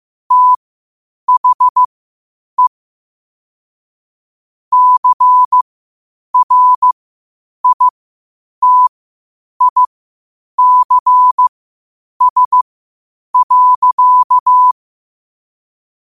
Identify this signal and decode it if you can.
Morse code, heard as a keyed tone, about 15 wpm, spaced as for 8 wpm, 1000 Hz. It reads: THE CRITICS.
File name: qotdfiles / 8wpm